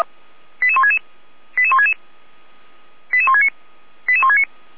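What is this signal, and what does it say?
Five-tone selective call (ZVEI), heard as quick runs of five tones, tones of about 70 ms. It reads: 88166 88166 88169 88169
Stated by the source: ZVEI five-tone selective call received by SDR.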